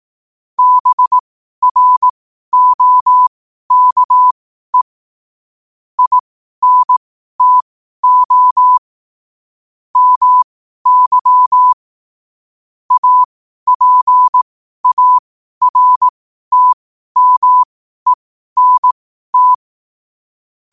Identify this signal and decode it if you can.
Morse code, heard as a keyed tone, about 18 wpm, spaced as for 12 wpm, 1000 Hz. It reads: BROKE INTO MY APARTMENT
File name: qotdfiles / 12wpm